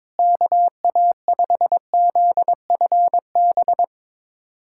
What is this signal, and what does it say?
Morse code, heard as a keyed tone, about 22 wpm, 700 Hz. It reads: KA5ZFB
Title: USA Callsign DE Callsign - 6 Characters - 22wpm